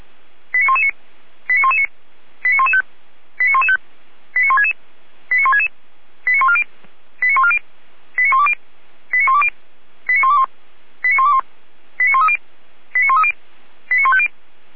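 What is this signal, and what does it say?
Five-tone selective call (ZVEI), heard as quick runs of five tones, tones of about 80 ms. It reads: 89119 89119 89115 89115 89166 89166 89140 89140 89120 89120 89121 89121 89130 89130 89150
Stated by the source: ZVEI five-tone selective call 89072 received on SDR.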